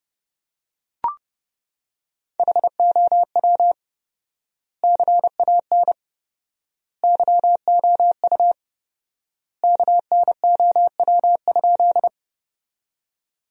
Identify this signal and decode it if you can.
Morse code, heard as a keyed tone, about 30 wpm, 700 Hz. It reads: HOW CAN YOU KNOW?
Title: 4x Word-Spacing - Sentences from Top 100 Words - 30wpm